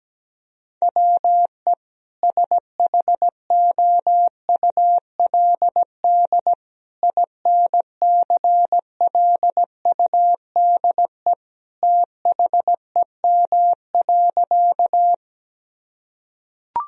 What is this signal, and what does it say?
Morse code, heard as a keyed tone, about 17 wpm, 700 Hz. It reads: WE SHOULD INCLUDE THEM.
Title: Sentences from Top 1000 Words - Review Part 3 - 17wpm